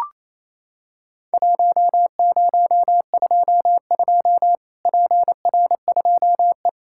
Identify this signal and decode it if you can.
Morse code, heard as a keyed tone, about 28 wpm, 700 Hz. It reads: 1022 PR2E
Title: Worked All Europe DX Contest - QTC Exchange - 28wpm